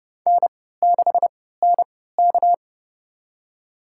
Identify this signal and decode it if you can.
Morse code, heard as a keyed tone, about 30 wpm, spaced as for 17 wpm, 700 Hz. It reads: N6NK
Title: USA Call Signs - 3x Character Spacing - 30wpm